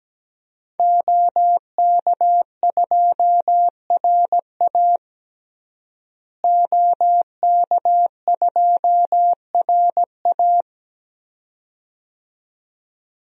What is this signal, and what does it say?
Morse code, heard as a keyed tone, about 17 wpm, 700 Hz. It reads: OK2RA OK2RA
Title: DXCC-100 Call Signs - 5 Characters - Repeated 2x - 17wpm